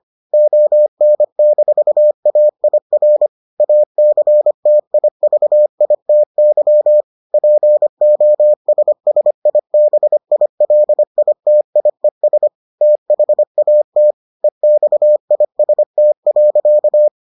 Morse code, heard as a keyed tone, about 25 wpm, 600 Hz. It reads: ON-AIR ACTIVITY POSSIBILITIES THAT EXIST.